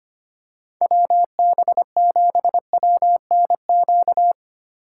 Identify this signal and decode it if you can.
Morse code, heard as a keyed tone, about 25 wpm, 700 Hz. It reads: WB7WNQ
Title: USA Call Signs - 6 Characters - 6x Character Spacing - 25wpm